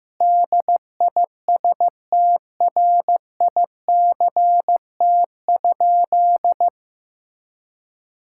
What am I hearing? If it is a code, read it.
Morse code, heard as a keyed tone, about 15 wpm, 700 Hz. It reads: DISTRICT?
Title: Sentences from Top 1600 Words 15wpm